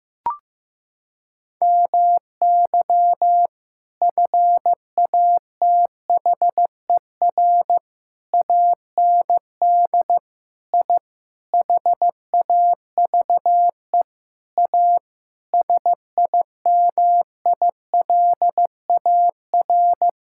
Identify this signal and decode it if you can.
Morse code, heard as a keyed tone, about 15 wpm, 700 Hz. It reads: MY FATHER AND I HAVE A SIMILAR
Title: Sentences from Top 1000 Words - Review Part 6 - 15wpm